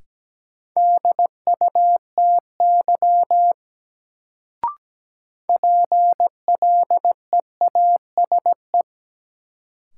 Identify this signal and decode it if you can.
Morse code, heard as a keyed tone, about 17 wpm, 700 Hz. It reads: DUTY PLEASE
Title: Sentences from Top 1500 Words - New Word Drill Alt - 17wpm